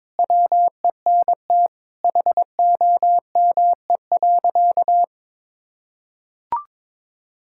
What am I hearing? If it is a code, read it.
Morse code, heard as a keyed tone, about 22 wpm, 700 Hz. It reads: WENT HOME.
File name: practice / 22wpm / Sentences with 4-Word Phrases - 22wpm